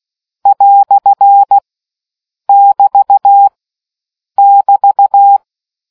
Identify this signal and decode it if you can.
Morse code, heard as a keyed tone, about 16 wpm, 800 Hz. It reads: " <BT> <BT>